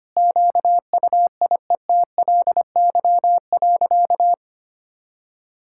Morse code, heard as a keyed tone, about 25 wpm, 690 Hz. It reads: QUIETLY.
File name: practice / 25wpm / Sentences with 2-Word Phrases - 25wpm